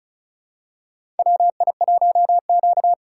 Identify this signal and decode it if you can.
Morse code, heard as a keyed tone, about 36 wpm, 700 Hz. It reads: WI1Q
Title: USA Call Signs - 4 Characters - 2x Character Spacing - 35wpm